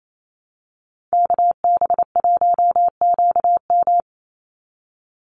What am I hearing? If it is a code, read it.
Morse code, heard as a keyed tone, about 28 wpm, 700 Hz. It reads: KB1QM
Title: USA Call Signs 28wpm